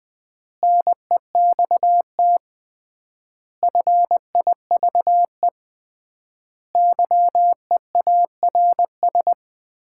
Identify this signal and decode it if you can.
Morse code, heard as a keyed tone, about 20 wpm, 700 Hz. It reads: NEXT FIVE YEARS
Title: 3x Word-Spacing - Sets of 5 Words - 20wpm